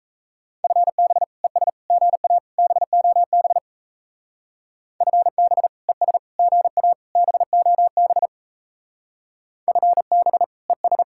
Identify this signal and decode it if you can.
Morse code, heard as a keyed tone, about 34 wpm, 700 Hz. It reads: FB ES GA BOB FB ES GA BOB FB ES
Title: Speed-Racing Common QSO Phrases - 28wpm